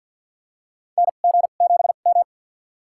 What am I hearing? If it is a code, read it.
Morse code, heard as a keyed tone, about 53 wpm, spaced as for 37 wpm, 700 Hz. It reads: NG7M
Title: USA Call Signs - 2x Character Spacing - 50wpm